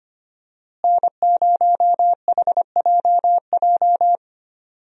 Morse code, heard as a keyed tone, about 25 wpm, 700 Hz. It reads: N0HJJ
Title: USA Call Signs - 5 Characters - Repeated 2x - 25wpm